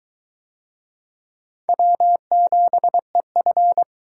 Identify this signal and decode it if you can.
Morse code, heard as a keyed tone, about 19 wpm, 700 Hz. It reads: W7EF